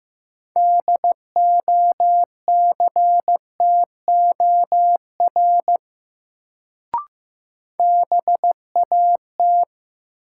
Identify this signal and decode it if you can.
Morse code, heard as a keyed tone, about 15 wpm, 700 Hz. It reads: DOCTOR BAT
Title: Sentences from Top 1500 Words - New Word Drill Alt - 15wpm